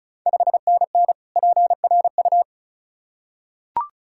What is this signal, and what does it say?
Morse code, heard as a keyed tone, about 36 wpm, 690 Hz. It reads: 5NN PRU
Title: Ontario QSO Party Exchange 35wpm